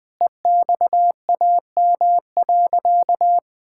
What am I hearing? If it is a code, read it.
Morse code, heard as a keyed tone, about 20 wpm, 700 Hz. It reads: EXAM.
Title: Sentences from Top 1000 Words - Review Part 5 - 20wpm